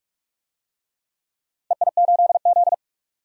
Morse code, heard as a keyed tone, about 42 wpm, 700 Hz. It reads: EI8Z
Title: DXCC-100 Call Signs - 4 Characters - Repeated 3x - 45wpm